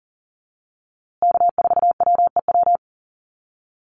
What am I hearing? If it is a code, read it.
Morse code, heard as a keyed tone, about 40 wpm, 700 Hz. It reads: K4WEW
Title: USA Call Signs Repeated 2x 40wpm